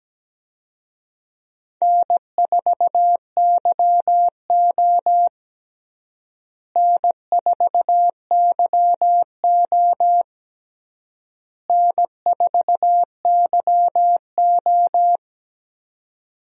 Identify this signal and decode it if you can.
Morse code, heard as a keyed tone, about 17 wpm, 700 Hz. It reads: N4YO N4YO N4YO
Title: USA Call Signs - 4 Characters - Repeated 3x - 17wpm